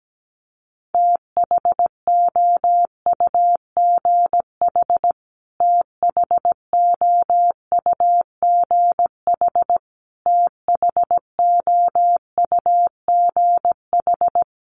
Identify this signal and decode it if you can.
Morse code, heard as a keyed tone, about 17 wpm, 700 Hz. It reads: THOUGH THOUGH THOUGH
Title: Top 500 Words Repeated 17wpm